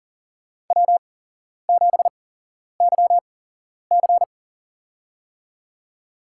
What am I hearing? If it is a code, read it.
Morse code, heard as a keyed tone, about 40 wpm, spaced as for 11 wpm, 700 Hz. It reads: W7YC
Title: USA Call Signs - 4 Characters - 8x Character Spacing - 40wpm